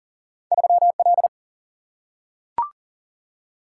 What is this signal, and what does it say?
Morse code, heard as a keyed tone, about 40 wpm, 700 Hz. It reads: 3 L
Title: Single-Letter Single-Number 40wpm